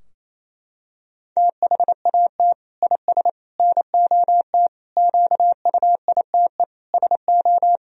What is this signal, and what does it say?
Morse code, heard as a keyed tone, about 28 wpm, 700 Hz. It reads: THAT IS NOT QUITE SO
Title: Sentences from Top 1000 Words - Review Part 3 - 28wpm